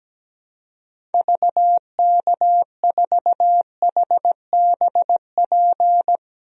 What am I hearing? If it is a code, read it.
Morse code, heard as a keyed tone, about 17 wpm, 700 Hz. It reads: VK4HBP
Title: DXCC-100 Call Signs - 6 Characters - 17wpm